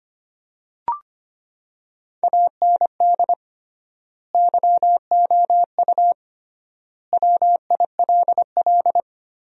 Morse code, heard as a keyed tone, about 25 wpm, 700 Hz. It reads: AND YOU WILL
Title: 3x Word-Spacing - Sentences from Top 400 Words - 25wpm